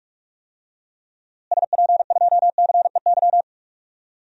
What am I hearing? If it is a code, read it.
Morse code, heard as a keyed tone, about 46 wpm, 700 Hz. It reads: SP2CEY
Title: DXCC-100 Call Signs - 6 Characters - Repeated 3x - 45wpm